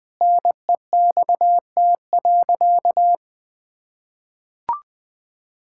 Morse code, heard as a keyed tone, about 21 wpm, 700 Hz. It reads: NEXT.
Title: Sentences with 7-Word Phrases - 20wpm